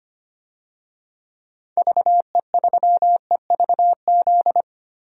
Morse code, heard as a keyed tone, about 25 wpm, 700 Hz. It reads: VE3EVZ
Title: DXCC-100 Call Signs - 6 Characters - Repeated 3x - 25wpm